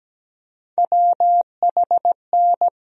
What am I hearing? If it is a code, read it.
Morse code, heard as a keyed tone, about 17 wpm, 700 Hz. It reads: WHN
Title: USA Call Sign Postfixes - 3 Letters 17wpm